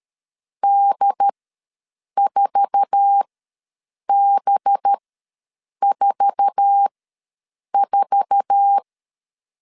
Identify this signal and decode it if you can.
Morse code, heard as a keyed tone, about 13 wpm, spaced as for 7 wpm, 800 Hz. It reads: D4B44
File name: LEZIONI / 8 YJ45 / 8 YJ4 ws 12wpm 3249